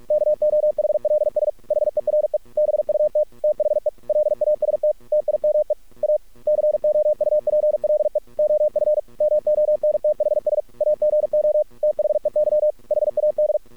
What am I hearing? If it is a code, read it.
Morse code, heard as a keyed tone, about 48 wpm, 610 Hz. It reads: KOHLS HERE BUT THE LAST TIME A COUPLE OF MONTHS AGO THEY HAD
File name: archived / CRSnet